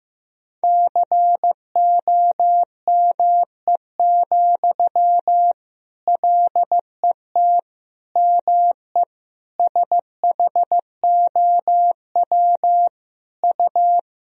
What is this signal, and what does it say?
Morse code, heard as a keyed tone, about 15 wpm, 700 Hz. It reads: COME, LET ME SHOW U
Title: Sentences from Top 500 Words and Abbreviations 15wpm